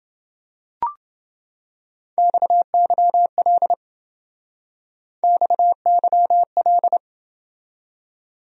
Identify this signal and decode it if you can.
Morse code, heard as a keyed tone, about 28 wpm, 700 Hz. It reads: XYL XYL